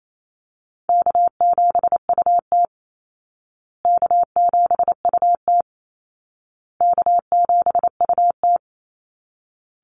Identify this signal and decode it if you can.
Morse code, heard as a keyed tone, about 28 wpm, 700 Hz. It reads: K7UT K7UT K7UT
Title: Easy Call Signs Repeated 3x - 28wpm